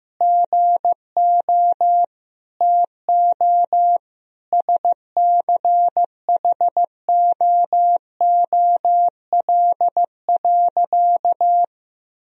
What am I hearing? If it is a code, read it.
Morse code, heard as a keyed tone, about 16 wpm, 700 Hz. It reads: GO TO SCHOOL.
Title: Sentences from Top 1000 Words - Review Part 1 - 15wpm